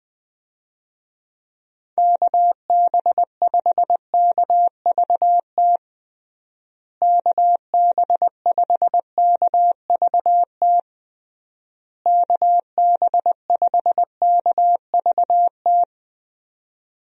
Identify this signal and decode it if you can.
Morse code, heard as a keyed tone, about 20 wpm, 700 Hz. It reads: KB5KVT KB5KVT KB5KVT
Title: USA Call Signs - 6 Characters - Repeated 3x - 20wpm